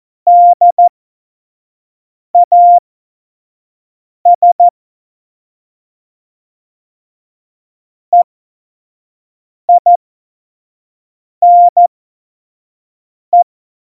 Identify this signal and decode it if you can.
Morse code, heard as a keyed tone, about 14 wpm, spaced as for 5 wpm, 700 Hz. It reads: DAS EINE
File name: Morsekurs / Witze / Witz 27. November 2018, 25 F70